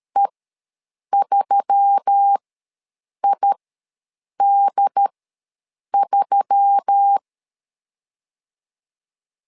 Morse code, heard as a keyed tone, about 13 wpm, spaced as for 7 wpm, 800 Hz. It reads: E3ID3
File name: LEZIONI / 4 Q1S3 ws 12wpm 2038